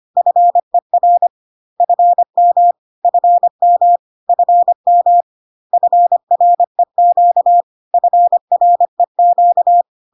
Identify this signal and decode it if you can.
Morse code, heard as a keyed tone, about 25 wpm, 700 Hz. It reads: FER FM FM FM FREQ FREQ
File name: CW abbreviations alphabetical repeated 3 times 25 25wpm